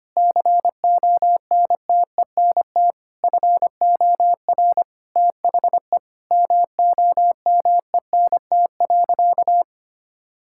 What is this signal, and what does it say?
Morse code, heard as a keyed tone, about 25 wpm, 700 Hz. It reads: CONTENT FOR THE MOMENT.
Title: Sentences from Top 1000 Words - Review Part 6 - 25wpm